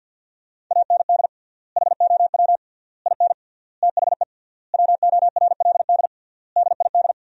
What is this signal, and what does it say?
Morse code, heard as a keyed tone, about 50 wpm, 700 Hz. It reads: AND HOW IN THE WORLD DID
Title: 3x Word-Spacing - Sentences from Top 200 Words - 50wpm